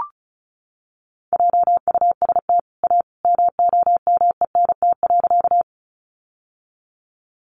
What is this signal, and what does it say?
Morse code, heard as a keyed tone, about 34 wpm, 700 Hz. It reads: JUST A MOMENT.
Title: Sentences from Top 1000 Words 35wpm